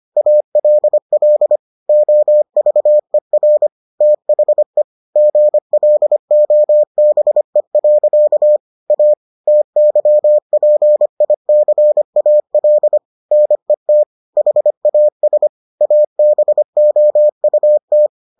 Morse code, heard as a keyed tone, about 25 wpm, 600 Hz. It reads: ALL OVER THE GLOBE. A TYPICAL NET HAS ABOUT